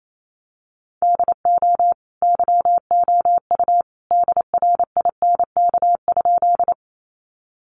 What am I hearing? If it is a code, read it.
Morse code, heard as a keyed tone, about 28 wpm, 700 Hz. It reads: DO YOU DRINK?
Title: Sentences from Top 1500 Words 28wpm